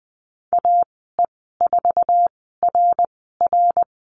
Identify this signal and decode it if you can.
Morse code, heard as a keyed tone, about 20 wpm, spaced as for 15 wpm, 700 Hz. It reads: AE4RR
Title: 062 Lesson 11g - Letter h - Call Signs 20wpm